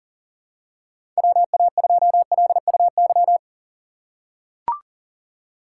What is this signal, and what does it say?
Morse code, heard as a keyed tone, about 40 wpm, 700 Hz. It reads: WA2LUY
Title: USA Call Signs - 6 Characters - 3x Character Spacing - 40wpm